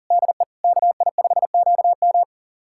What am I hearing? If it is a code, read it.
Morse code, heard as a keyed tone, about 42 wpm, 700 Hz. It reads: DE KI5QM
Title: USA Callsign DE Callsign 40wpm